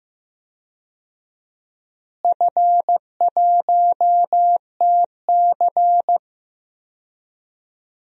Morse code, heard as a keyed tone, about 15 wpm, 700 Hz. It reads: F1TC
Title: DXCC-100 Call Signs - 4 Characters - Repeated 2x - 15wpm